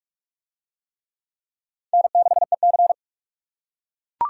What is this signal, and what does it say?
Morse code, heard as a keyed tone, about 41 wpm, 700 Hz. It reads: N6EC